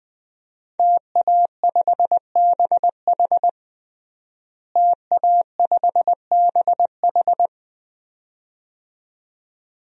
Morse code, heard as a keyed tone, about 20 wpm, 700 Hz. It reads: TA5BH TA5BH
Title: DXCC-100 Call Signs - 5 Characters - Repeated 2x - 20wpm